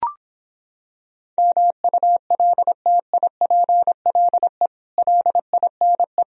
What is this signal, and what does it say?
Morse code, heard as a keyed tone, about 26 wpm, 700 Hz. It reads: MULTIPLE LINE